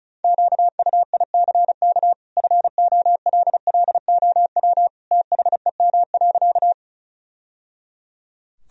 Morse code, heard as a keyed tone, about 36 wpm, 700 Hz. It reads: QUICK FOLLOW THEM.
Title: Sentences from Top 1000 Words - Review Part 1 - 35wpm